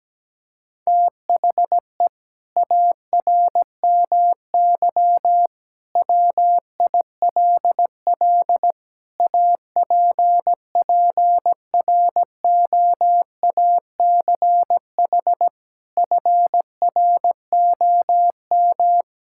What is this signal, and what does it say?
Morse code, heard as a keyed tone, about 17 wpm, 700 Hz. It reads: THE ARMY WILL APPROACH FROM
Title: Sentences from Top 1000 Words - Review Part 6 - 17wpm